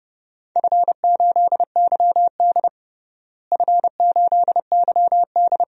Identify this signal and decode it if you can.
Morse code, heard as a keyed tone, about 30 wpm, 700 Hz. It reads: F8YD F8YD
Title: DXCC-100 Call Signs - 4 Characters - Repeated 2x - 30wpm